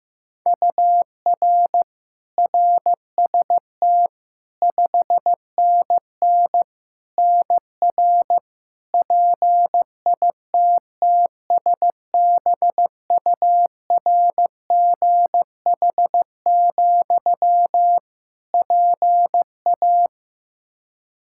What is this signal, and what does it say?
Morse code, heard as a keyed tone, about 15 wpm, 700 Hz. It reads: UR RST 5NN NR PITTSBURGH, PA
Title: Practice QSO - Exchange 1 - Beginning - 15wpm